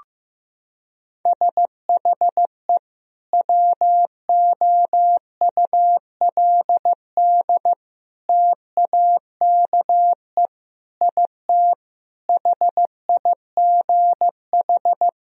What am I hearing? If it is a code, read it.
Morse code, heard as a keyed tone, about 15 wpm, 700 Hz. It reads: SHE WOULD TAKE IT HIGH
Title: Sentences from Top 1000 Words - Review Part 1 - 15wpm